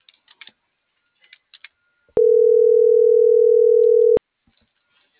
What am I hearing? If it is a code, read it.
Ringback tone (North America), 440 + 480 Hz.